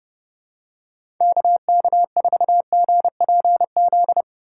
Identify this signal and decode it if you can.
Morse code, heard as a keyed tone, about 31 wpm, 700 Hz. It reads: KK4GPZ
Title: USA Call Signs - 6 Characters - 4x Character Spacing - 30wpm